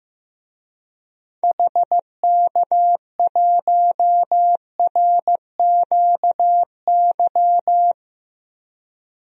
Morse code, heard as a keyed tone, about 15 wpm, 700 Hz. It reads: HK1RQY
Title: DXCC-100 Call Signs - 6 Characters - Repeated 2x - 15wpm